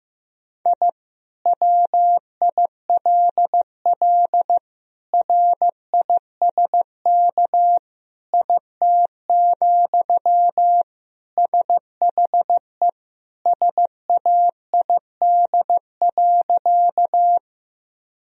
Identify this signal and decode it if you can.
Morse code, heard as a keyed tone, about 15 wpm, 700 Hz. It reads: I WILL RISK IT, SHE SAID.
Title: Sentences from Top 1000 Words - Review Part 4 - 15wpm